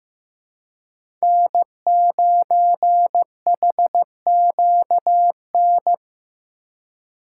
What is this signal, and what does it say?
Morse code, heard as a keyed tone, about 15 wpm, 700 Hz. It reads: N9HQN
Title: USA Call Signs - 5 Characters - 6x Character Spacing - 15wpm